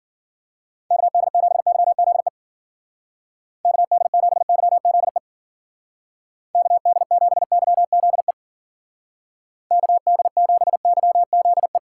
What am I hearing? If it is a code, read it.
Morse code, heard as a keyed tone, about 53 wpm, 700 Hz. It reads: KD7YZE KD7YZE KD7YZE KD7YZE